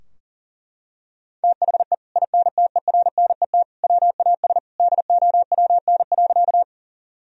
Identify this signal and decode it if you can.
Morse code, heard as a keyed tone, about 39 wpm, 700 Hz. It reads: THE INTERNET WAS DOWN.
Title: Sentences from Top 1000 Words - Review Part 4 - 40wpm